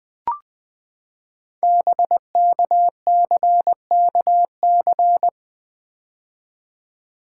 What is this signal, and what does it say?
Morse code, heard as a keyed tone, about 20 wpm, 700 Hz. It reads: BKCKC
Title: Code Groups - 5 Letters - 20wpm